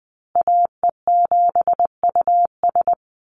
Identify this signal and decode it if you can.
Morse code, heard as a keyed tone, about 20 wpm, 700 Hz. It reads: AE7US
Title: Call Signs Repeated 3x - 20wpm